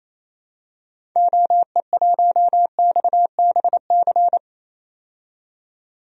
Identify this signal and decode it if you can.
Morse code, heard as a keyed tone, about 28 wpm, 700 Hz. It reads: OE1XBC
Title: DXCC-100 Call Signs - 6 Characters - 28wpm